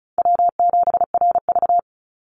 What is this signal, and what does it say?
Morse code, heard as a keyed tone, about 36 wpm, 700 Hz. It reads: W7RV
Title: NA SOTA Chaser Callsigns 35wpm